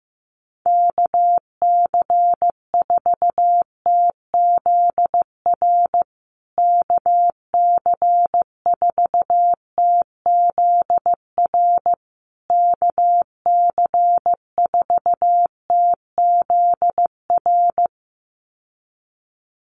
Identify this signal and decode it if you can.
Morse code, heard as a keyed tone, about 15 wpm, 700 Hz. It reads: KC4TZR KC4TZR KC4TZR
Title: USA Call Signs Repeated 3x 15wpm